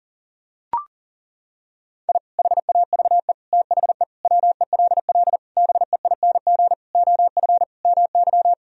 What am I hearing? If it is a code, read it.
Morse code, heard as a keyed tone, about 37 wpm, 700 Hz. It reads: I HAVE THE WELL BEING OF MY
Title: Sentences from Top 1000 Words - Review Part 3 - 40wpm